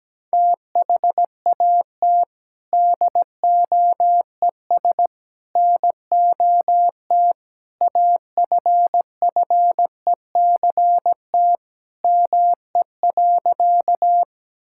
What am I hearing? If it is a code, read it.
Morse code, heard as a keyed tone, about 17 wpm, 700 Hz. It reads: THAT DOES NOT AFFECT ME.
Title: Sentences from Top 1000 Words - Review Part 6 - 17wpm